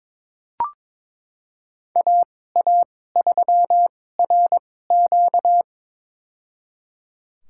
Morse code, heard as a keyed tone, about 22 wpm, spaced as for 16 wpm, 700 Hz. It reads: AA3RQ
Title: USA Call Signs - 2x Character Spacing - 22wpm